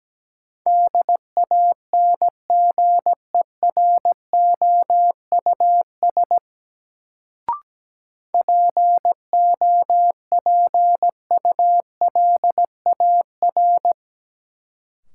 Morse code, heard as a keyed tone, about 17 wpm, 700 Hz. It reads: DANGEROUS POPULAR
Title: Sentences from Top 1200 Words - New Word Drill Alt - 17wpm